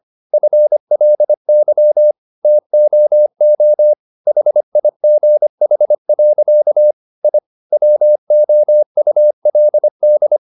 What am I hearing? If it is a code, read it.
Morse code, heard as a keyed tone, about 25 wpm, 600 Hz. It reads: FLY TOO HIGH. I WOULD